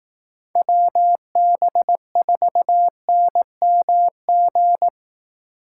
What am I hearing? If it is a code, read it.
Morse code, heard as a keyed tone, about 18 wpm, 700 Hz. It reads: WB4NMG
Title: Speed Racing - USA 2x3 Call Signs 15wpm